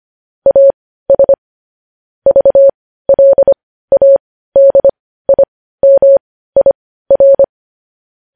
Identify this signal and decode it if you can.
Morse code, heard as a keyed tone, about 25 wpm, spaced as for 15 wpm, 550 Hz. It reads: AS VLADIMIR
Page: news-headlines-in-morse-code-at-15-wpm